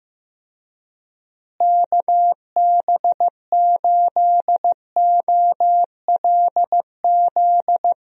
Morse code, heard as a keyed tone, about 15 wpm, 700 Hz. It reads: KB8OLZ
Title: USA Call Signs - 6 Characters - Repeated 3x - 15wpm